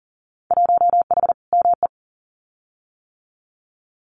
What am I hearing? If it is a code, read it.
Morse code, heard as a keyed tone, about 40 wpm, 700 Hz. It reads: 1H ME
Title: Winter Field Day Exchanges 40wpm